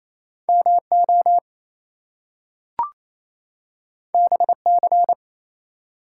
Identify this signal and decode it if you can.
Morse code, heard as a keyed tone, about 29 wpm, 700 Hz. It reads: MO BC
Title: US States and Canadian Provinces 28wpm